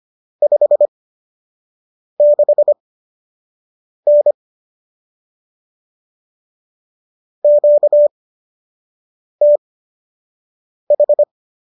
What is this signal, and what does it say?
Morse code, heard as a keyed tone, about 25 wpm, spaced as for 6 wpm, 590 Hz. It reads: 56N QTH